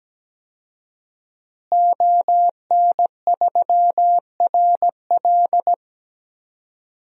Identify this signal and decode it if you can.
Morse code, heard as a keyed tone, about 17 wpm, 700 Hz. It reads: ON3RL
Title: DXCC-100 Call Signs - 5 Characters - Repeated 3x - 17wpm